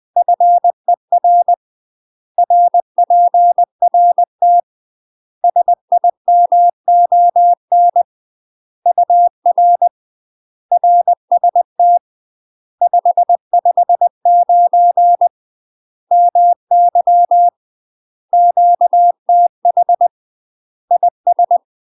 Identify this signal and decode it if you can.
Morse code, heard as a keyed tone, about 20 wpm, 700 Hz. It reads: FER RPRT SIMON UR RST 559 MY QTH IS
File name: UK MASTER QSOs TRAINING / 20WPM / UK MASTER QSOs 0006